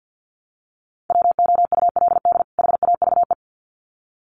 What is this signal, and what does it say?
Morse code, heard as a keyed tone, about 50 wpm, 700 Hz. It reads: WOULD HAVE
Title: Sets of 2 Words 50wpm